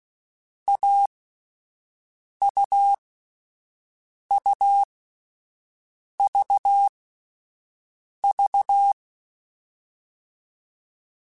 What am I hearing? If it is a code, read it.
Morse code, heard as a keyed tone, about 16 wpm, 800 Hz. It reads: A U U V V
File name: Difficult Combos / Difficult Combos-mp3s / AUV AUV4 @10